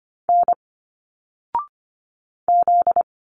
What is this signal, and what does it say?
Morse code, heard as a keyed tone, about 25 wpm, 700 Hz. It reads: N Z
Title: Single Letters 25wpm